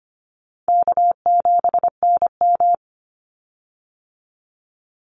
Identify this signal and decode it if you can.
Morse code, heard as a keyed tone, about 25 wpm, 700 Hz. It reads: K7NM
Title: Easy Call Signs Repeated 2x - 25wpm